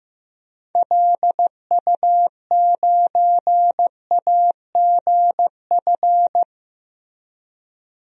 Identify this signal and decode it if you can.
Morse code, heard as a keyed tone, about 15 wpm, 700 Hz. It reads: LU9AGF
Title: DXCC-100 Call Signs - 6 Characters - Repeated 2x - 15wpm